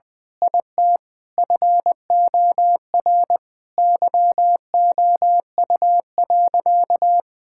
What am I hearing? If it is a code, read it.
Morse code, heard as a keyed tone, about 20 wpm, 700 Hz. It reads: IT FOR YOU.
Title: Sentences from Top 1000 Words - Review Part 6 - 20wpm